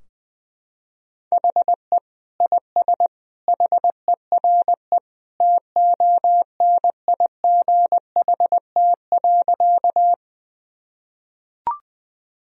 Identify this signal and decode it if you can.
Morse code, heard as a keyed tone, about 20 wpm, 700 Hz. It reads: HE IS HERE TONIGHT.
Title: Sentences from Top 1000 Words - Review Part 3 - 20wpm